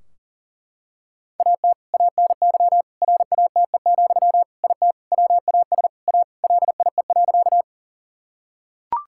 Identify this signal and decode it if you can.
Morse code, heard as a keyed tone, about 40 wpm, 700 Hz. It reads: AT ANY RATE, IT WAS A LIE.
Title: Sentences from Top 1600 Words 40wpm